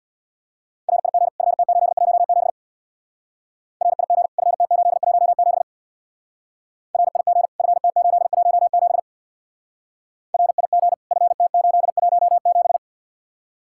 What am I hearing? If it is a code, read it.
Morse code, heard as a keyed tone, about 63 wpm, 700 Hz. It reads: RIG FT817 RIG FT817 RIG FT817 RIG FT817